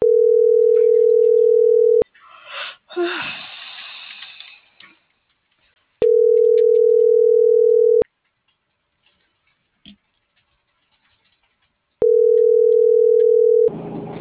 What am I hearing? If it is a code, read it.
Ringback tone (North America), 440 + 480 Hz.